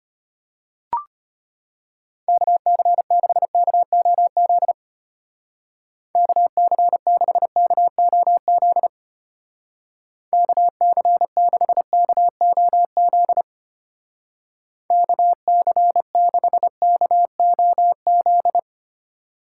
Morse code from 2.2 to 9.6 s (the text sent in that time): KC6KOZ KC6KOZ